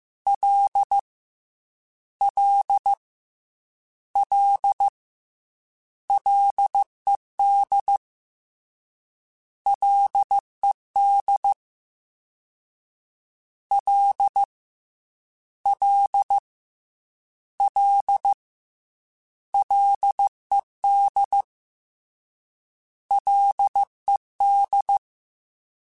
Morse code, heard as a keyed tone, about 15 wpm, 800 Hz. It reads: L L L LED LED L L L LED LED
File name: Ls / Single L m3s / Single L Prefixe 3x2 Twice @13